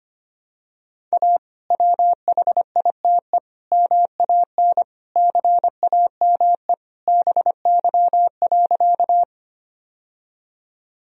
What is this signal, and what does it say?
Morse code, heard as a keyed tone, about 25 wpm, 700 Hz. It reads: A WHITE MAN CAME BY.
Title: Sentences from Top 1000 Words - Review Part 1 - 25wpm